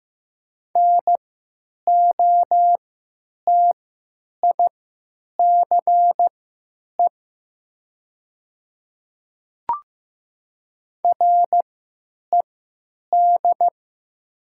Morse code, heard as a keyed tone, about 15 wpm, spaced as for 9 wpm, 700 Hz. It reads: NOTICE RED
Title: 3x Character Spacing - Top 500 Words - 15wpm